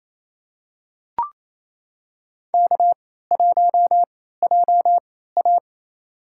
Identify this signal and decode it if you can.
Morse code, heard as a keyed tone, about 28 wpm, spaced as for 16 wpm, 700 Hz. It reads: K1JA
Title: USA Call Signs - 3x Character Spacing - 28wpm